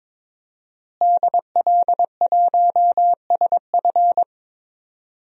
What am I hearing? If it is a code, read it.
Morse code, heard as a keyed tone, about 22 wpm, 700 Hz. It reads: DL1SF
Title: DXCC-100 Call Signs - 5 Characters - Repeated 2x - 22wpm